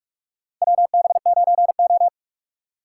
Morse code, heard as a keyed tone, about 46 wpm, 700 Hz. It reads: WB9O
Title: USA Call Signs - 3x Character Spacing - 45wpm